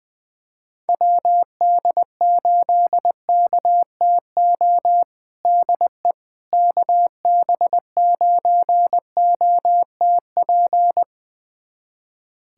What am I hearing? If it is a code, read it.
Morse code, heard as a keyed tone, about 20 wpm, 700 Hz. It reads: WD8KTO DE KB9OTP
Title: USA Callsign DE Callsign - 6 Characters - 20wpm